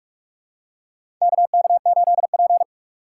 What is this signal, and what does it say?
Morse code, heard as a keyed tone, about 46 wpm, 700 Hz. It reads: KK8P
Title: USA Call Signs - 4 Characters - 6x Character Spacing - 45wpm